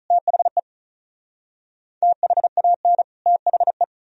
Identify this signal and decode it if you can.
Morse code, heard as a keyed tone, about 43 wpm, 700 Hz. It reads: THE THAN THE
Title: Speed-Racing - Sets of 2 Words 35wpm